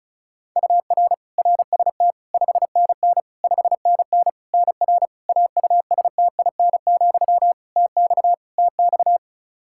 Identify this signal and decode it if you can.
Morse code, heard as a keyed tone, about 36 wpm, 700 Hz. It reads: UR RST 5NN 5NN NR AUSTIN, TX TX
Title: Practice QSO - Exchange 1 - Beginning - 35wpm